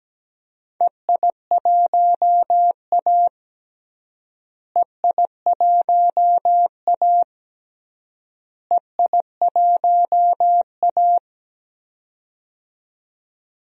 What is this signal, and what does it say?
Morse code, heard as a keyed tone, about 17 wpm, 700 Hz. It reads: EI1A EI1A EI1A
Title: DXCC-100 Call Signs - 4 Characters - Repeated 3x - 17wpm